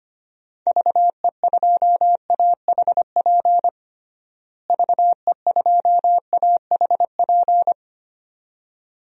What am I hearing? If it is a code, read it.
Morse code, heard as a keyed tone, about 25 wpm, 700 Hz. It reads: VE2AHP VE2AHP
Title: DXCC-100 Call Signs - 6 Characters - Repeated 2x - 25wpm